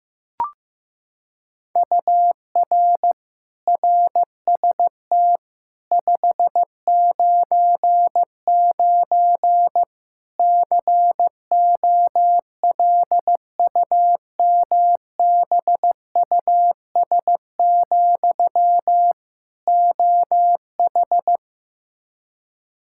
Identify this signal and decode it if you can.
Morse code, heard as a keyed tone, about 15 wpm, 700 Hz. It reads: UR RST 599 COLUMBUS, OH
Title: K1USN SST QSO Practice 15wpm